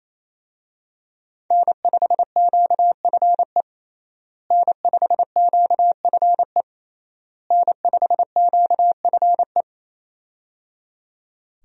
Morse code, heard as a keyed tone, about 28 wpm, 700 Hz. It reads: N5QFE N5QFE N5QFE
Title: USA Call Signs - 5 Characters - Repeated 3x - 28wpm